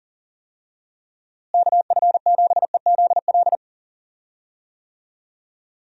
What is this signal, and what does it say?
Morse code, heard as a keyed tone, about 37 wpm, 700 Hz. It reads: KF7EZL
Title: USA Call Signs - 6 Characters - Repeated 2x - 40wpm